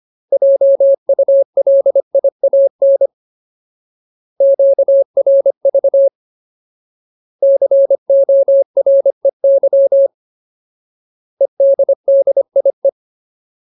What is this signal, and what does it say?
Morse code, heard as a keyed tone, about 25 wpm, 550 Hz. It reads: JULIAN QRV COREY EDDIE
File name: Ultimate Word List / 25WPM Ultimate